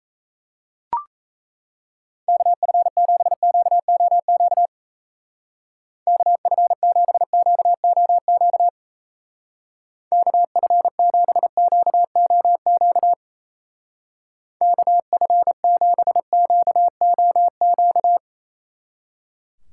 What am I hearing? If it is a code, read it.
Morse code, heard as a keyed tone, about 34 wpm, 700 Hz. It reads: KF7QOQ KF7QOQ KF7QOQ KF7QOQ
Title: Speed Racing - USA 2x3 Call Signs 28wpm